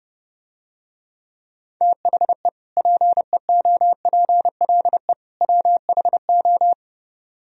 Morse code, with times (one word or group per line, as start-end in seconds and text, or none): 1.8-2.5 THE
2.8-5.1 PEOPLE
5.4-6.7 WHO